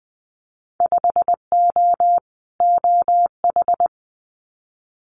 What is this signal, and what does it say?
Morse code, heard as a keyed tone, about 20 wpm, 700 Hz. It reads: 5O OH
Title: Winter Field Day Exchanges Encore 20wpm